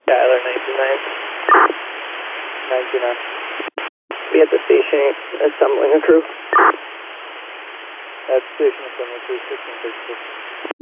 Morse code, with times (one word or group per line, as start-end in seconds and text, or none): none